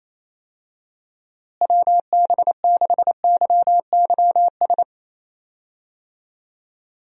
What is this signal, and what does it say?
Morse code, heard as a keyed tone, about 28 wpm, 700 Hz. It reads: WB6YYS
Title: USA Call Signs - 6 Characters - Repeated 2x - 28wpm